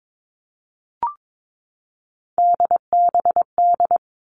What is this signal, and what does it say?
Morse code, heard as a keyed tone, about 22 wpm, 700 Hz. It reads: DBD
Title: All Cut Numbers - 3-Digit 22wpm